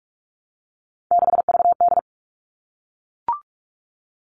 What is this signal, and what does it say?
Morse code, heard as a keyed tone, about 44 wpm, 700 Hz. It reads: 6VD
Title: All Cut Numbers - 3-Digit 45wpm